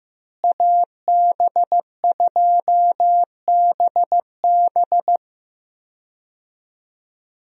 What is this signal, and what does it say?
Morse code, heard as a keyed tone, about 16 wpm, 700 Hz. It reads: AB2BB
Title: USA Call Signs - 5 Characters - Repeated 3x - 15wpm